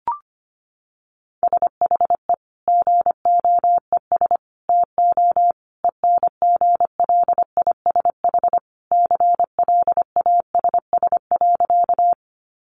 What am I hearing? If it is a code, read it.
Morse code, heard as a keyed tone, about 25 wpm, 700 Hz. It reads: SHE GOES TO ENGLISH CLASS.
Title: Additional Easy Phrases 25wpm